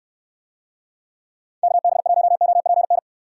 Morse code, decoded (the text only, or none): KB1OPN